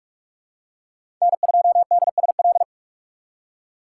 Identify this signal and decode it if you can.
Morse code, heard as a keyed tone, about 46 wpm, 700 Hz. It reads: N2DSL
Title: USA Call Signs - 5 Characters - 2x Character Spacing - 45wpm